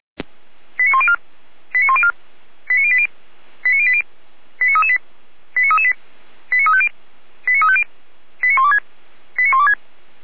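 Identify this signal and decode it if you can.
Five-tone selective call (ZVEI), heard as quick runs of five tones, tones of about 70 ms. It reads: 89104 89104 89080 89080 89338 89338 89350 89350 89126 89126
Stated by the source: ZVEI five-tone selective call received on an SDR, from the Feuerwehr Orlamünde audio server.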